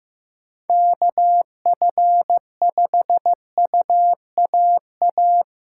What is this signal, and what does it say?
Morse code, heard as a keyed tone, about 16 wpm, 700 Hz. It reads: KF5UAA
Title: Speed Racing - USA 2x3 Call Signs 15wpm